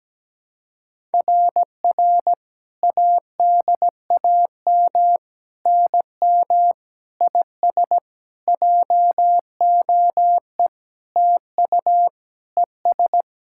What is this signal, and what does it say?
Morse code, heard as a keyed tone, about 17 wpm, 700 Hz. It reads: RR ADAM NM IS JOE TU ES
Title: SOTA QSO Practice 17wpm